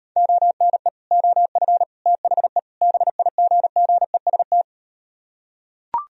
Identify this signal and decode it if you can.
Morse code, heard as a keyed tone, about 38 wpm, 700 Hz. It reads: ONE OF THE BIGGEST